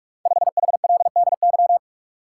Morse code, heard as a keyed tone, about 44 wpm, 690 Hz. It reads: 5HLDY
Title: Code Groups - 5 Letter:Numbers - 45wpm